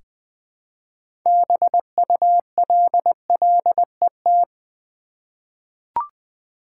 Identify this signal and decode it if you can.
Morse code, heard as a keyed tone, about 20 wpm, 700 Hz. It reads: BULLET
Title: 6 Letter Words - First 3 Letters Given - 20wpm